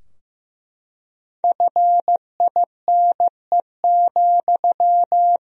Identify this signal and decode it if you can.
Morse code, heard as a keyed tone, about 16 wpm, 700 Hz. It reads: FINE,
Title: 3x Word-Spacing - Sentences from Top 500 Words - 15wpm